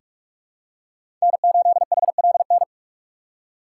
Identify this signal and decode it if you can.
Morse code, heard as a keyed tone, about 43 wpm, 700 Hz. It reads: N8HLN
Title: USA Call Signs - 5 Characters - 3x Character Spacing - 45wpm